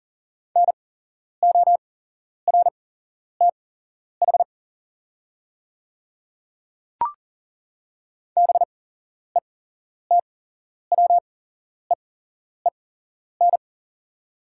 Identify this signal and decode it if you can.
Morse code, heard as a keyed tone, about 40 wpm, spaced as for 11 wpm, 700 Hz. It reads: NORTH BETWEEN
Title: 8x Character Spacing - Top 500 Words - 40wpm